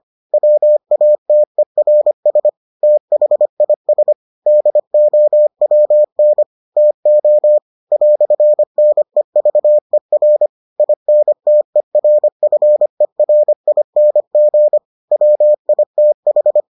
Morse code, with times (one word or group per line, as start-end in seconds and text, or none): 0.3-2.5 WATERS
2.8-4.1 THIS
4.5-6.4 DOWN
6.8-7.6 TO
7.9-10.5 "NEVER
10.8-14.8 INTERFERING
15.1-16.6 WITH